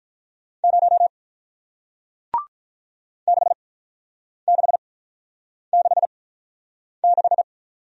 Morse code, heard as a keyed tone, about 46 wpm, spaced as for 9 wpm, 700 Hz. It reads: 0 6666